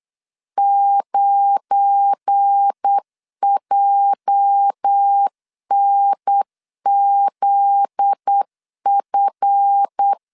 Morse code, heard as a keyed tone, about 8 wpm, 800 Hz. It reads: 9JNZF
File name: LEZIONI / 8 YJ45 / 8 YJ 8wpm 3313